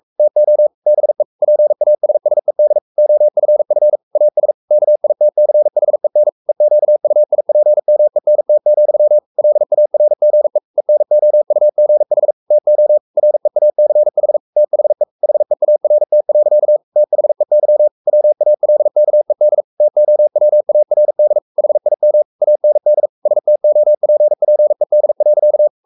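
Morse code, heard as a keyed tone, about 44 wpm, 600 Hz. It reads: TO BE PASSED OFF AS KITCHEN EQUIPMENT, LARGE ENOUGH TO REACH THE HEART. THEY WALKED TOWARD HIM AND STOPPED.
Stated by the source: Morse code at 43 wpm sending Ken Follett's novel Never in English, from CW med Gnister.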